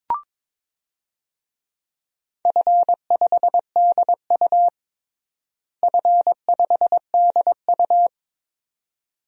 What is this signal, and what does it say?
Morse code, heard as a keyed tone, about 22 wpm, 700 Hz. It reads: F5DU F5DU
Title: DXCC-100 Call Signs - 4 Characters - Repeated 2x - 22wpm